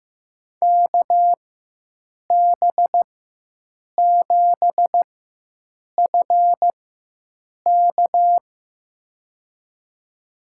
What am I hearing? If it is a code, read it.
Morse code, heard as a keyed tone, about 15 wpm, spaced as for 7 wpm, 700 Hz. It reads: KB7FK
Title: USA Call Signs - 5 Characters - 4x Character Spacing - 15wpm